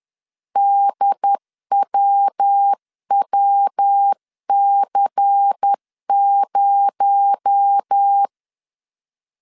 Morse code, heard as a keyed tone, about 10 wpm, 800 Hz. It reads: DWWC0
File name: CW / DET / 1 6 1500C S C 10wpm 4252